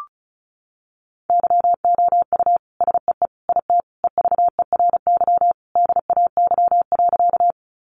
Morse code, heard as a keyed tone, about 34 wpm, 700 Hz. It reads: YOU SEE IT EVERY DAY.
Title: Sentences from Top 200 Words 35wpm